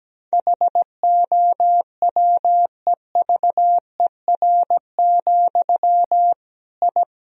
Morse code, heard as a keyed tone, about 17 wpm, 700 Hz. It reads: HOWEVER, I
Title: Sentences from Top 1000 Words - Review Part 3 - 17wpm